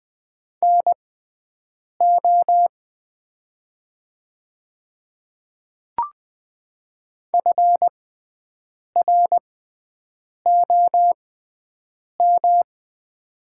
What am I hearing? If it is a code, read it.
Morse code, heard as a keyed tone, about 21 wpm, spaced as for 7 wpm, 700 Hz. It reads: NO FROM
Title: 6x Character Spacing - Top 100 Words - 20wpm